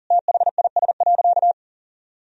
Morse code, heard as a keyed tone, about 40 wpm, 700 Hz. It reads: THIS.
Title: Sentences from Top 1000 Words - Review Part 2 - 40wpm